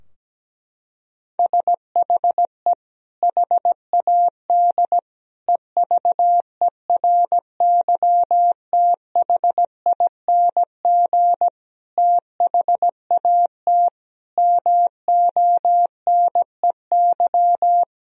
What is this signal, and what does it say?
Morse code, heard as a keyed tone, about 17 wpm, 700 Hz. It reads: SHE HAD EVERYTHING THAT MONEY
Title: Sentences from Top 1000 Words - Review Part 4 - 17wpm